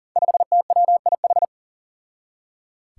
Morse code, heard as a keyed tone, about 38 wpm, 700 Hz. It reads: 5TWIH
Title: Five Challenging Letter-Number Combinations 40wpm